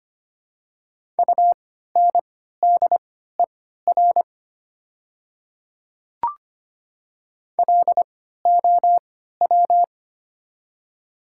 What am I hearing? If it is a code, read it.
Morse code, heard as a keyed tone, about 25 wpm, spaced as for 14 wpm, 700 Hz. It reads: UNDER LOW
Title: 3x Character Spacing - Top 200 Words - 25wpm